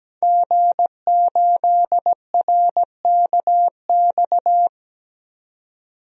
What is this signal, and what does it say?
Morse code, heard as a keyed tone, about 17 wpm, 700 Hz. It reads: G8RKX
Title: DXCC-100 Call Signs - 5 Characters - 17wpm